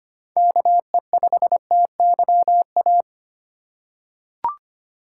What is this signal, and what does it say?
Morse code, heard as a keyed tone, about 25 wpm, 700 Hz. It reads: KE5TYA
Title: USA Call Signs - 6 Characters - 4x Character Spacing - 25wpm